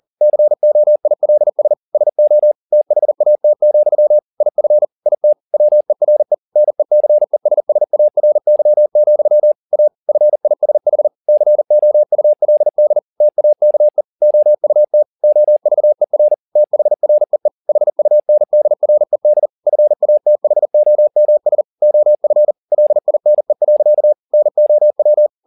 Morse code, heard as a keyed tone, about 40 wpm, 600 Hz. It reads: COILS SO THAT, IF IT WERE NECESSARY, A FISH COULD TAKE OUT OVER THREE HUNDRED FATHOMS OF LINE. NOW